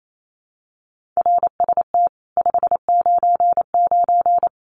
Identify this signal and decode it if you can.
Morse code, heard as a keyed tone, about 28 wpm, 700 Hz. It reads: RST 599
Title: Common QSO Phrases 28wpm